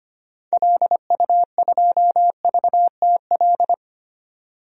Morse code, heard as a keyed tone, about 25 wpm, 700 Hz. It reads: LU2VTL
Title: DXCC-100 Call Signs - 6 Characters - 25wpm